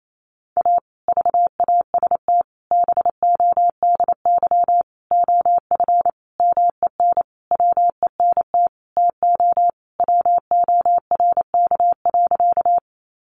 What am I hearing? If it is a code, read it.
Morse code, heard as a keyed tone, about 28 wpm, 700 Hz. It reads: A VAST BODY OF MEN WENT TO WORK.
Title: Sentences from Top 1300 Words 28wpm